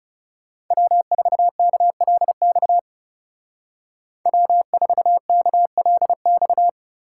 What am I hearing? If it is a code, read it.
Morse code, heard as a keyed tone, about 37 wpm, 700 Hz. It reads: W4KLX W4KLX
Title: Speed Racing - USA 1x3 Call Signs 30wpm